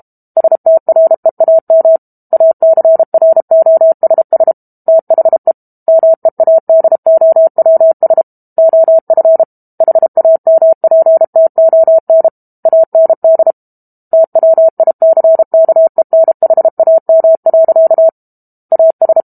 Morse code, heard as a keyed tone, about 32 wpm, 650 Hz. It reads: STREAM ACROSS THE MEADOWS OF HAMPTON AND TWICKENHAM. AS
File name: Book Two / 06-Chapter 6 - 32 WPM